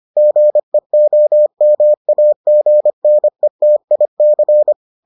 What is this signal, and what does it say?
Morse code, heard as a keyed tone, about 25 wpm, 600 Hz. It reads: GEOMAGNETIC